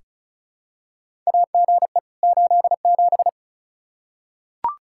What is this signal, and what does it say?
Morse code, heard as a keyed tone, about 34 wpm, 700 Hz. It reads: AGE 87
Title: K1USN SST QSO Practice 35wpm